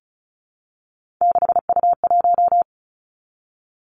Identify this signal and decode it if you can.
Morse code, heard as a keyed tone, about 36 wpm, 700 Hz. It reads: 6U1
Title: Three Challenging Letter-Number Combinations 35wpm